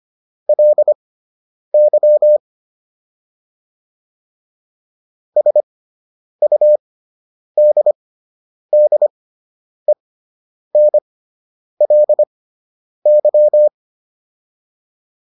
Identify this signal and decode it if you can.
Morse code, heard as a keyed tone, about 25 wpm, spaced as for 9 wpm, 600 Hz. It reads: LY SUDDENLY